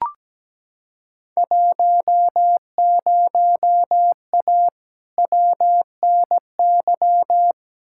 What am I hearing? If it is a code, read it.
Morse code, heard as a keyed tone, about 17 wpm, 700 Hz. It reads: 10A WNY
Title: Field Day Exchanges - 17wpm